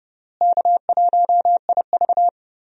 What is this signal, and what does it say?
Morse code, heard as a keyed tone, about 30 wpm, 700 Hz. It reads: K1IV
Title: USA Call Signs - 4 Characters - Repeated 3x - 30wpm